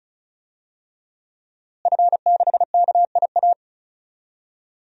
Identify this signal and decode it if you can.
Morse code, heard as a keyed tone, about 34 wpm, 700 Hz. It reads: F6KIA
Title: DXCC-100 Call Signs - 5 Characters - Repeated 3x - 35wpm